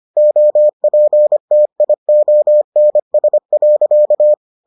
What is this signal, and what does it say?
Morse code, heard as a keyed tone, about 25 wpm, 600 Hz. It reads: OPTIONS.